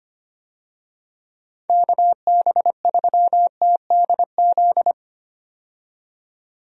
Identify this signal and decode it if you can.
Morse code, heard as a keyed tone, about 25 wpm, 700 Hz. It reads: KB3TDZ
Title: USA Call Signs - 6 Characters - 25wpm